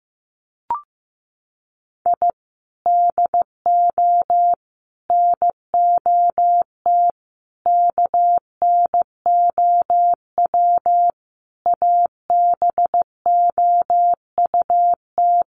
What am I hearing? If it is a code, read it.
Morse code, heard as a keyed tone, about 15 wpm, 700 Hz. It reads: I DO NOT KNOW ABOUT
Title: Sets of 5 Words 15wpm